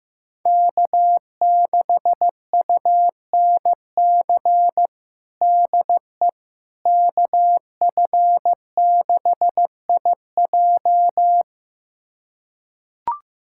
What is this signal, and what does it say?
Morse code, heard as a keyed tone, about 15 wpm, 700 Hz. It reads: K6UNC DE KF6IJ
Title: USA Callsign DE Callsign - 5 Characters - 15wpm